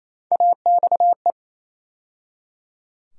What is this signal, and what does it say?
Morse code, heard as a keyed tone, about 27 wpm, 700 Hz. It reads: AXE
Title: USA Call Sign Postfixes - 3 Letters 28wpm